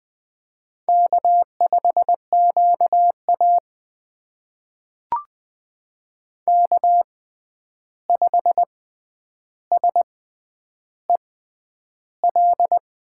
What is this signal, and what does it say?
Morse code, heard as a keyed tone, about 20 wpm, spaced as for 10 wpm, 700 Hz. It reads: K5QA K5SEL
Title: USA Call Signs - 6x Character Spacing - 20wpm